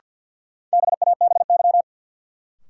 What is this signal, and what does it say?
Morse code, heard as a keyed tone, about 48 wpm, 700 Hz. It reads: BABY
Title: Sentences from Top 1300 Words - New Word Drill Alt - 50wpm